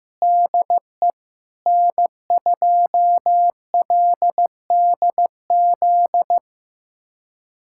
Morse code, heard as a keyed tone, about 16 wpm, 700 Hz. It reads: DE N2LDZ
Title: USA Callsign DE Callsign - 5 Characters - 15wpm